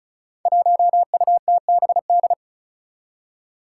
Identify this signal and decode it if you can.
Morse code, heard as a keyed tone, about 36 wpm, 700 Hz. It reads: 1UTBD
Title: Five Challenging Letter-Number Combinations 35wpm